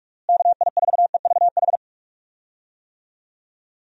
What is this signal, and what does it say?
Morse code, heard as a keyed tone, about 46 wpm, 700 Hz. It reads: KI4EVH
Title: USA Call Signs - 6 Characters - Repeated 3x - 45wpm